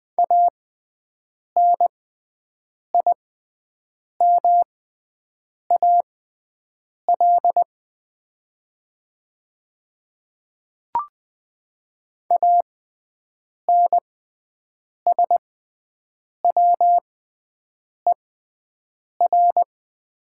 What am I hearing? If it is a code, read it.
Morse code, heard as a keyed tone, about 20 wpm, spaced as for 7 wpm, 700 Hz. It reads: ANIMAL ANSWER
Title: 6x Character Spacing - Top 300 Words - 20wpm